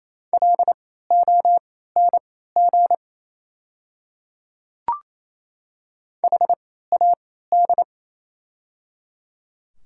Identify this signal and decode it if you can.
Morse code, heard as a keyed tone, about 28 wpm, spaced as for 16 wpm, 690 Hz. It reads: LONG HAD
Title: 3x Character Spacing - Top 100 Words - 28wpm